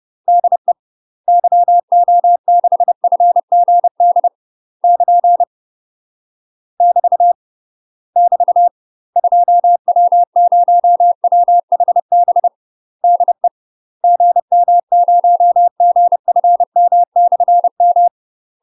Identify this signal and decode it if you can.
Morse code, heard as a keyed tone, about 30 wpm, 700 Hz. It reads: DE YO6FGD <KN> <BT> <BT> 2W0WHB DE GM0GFM/M
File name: UK MASTER QSOs TRAINING/30WPM